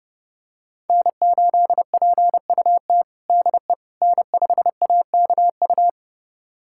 Morse code, heard as a keyed tone, about 30 wpm, 700 Hz. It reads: N8PUT DE N5AKU
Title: USA Callsign DE Callsign - 5 Characters - 30wpm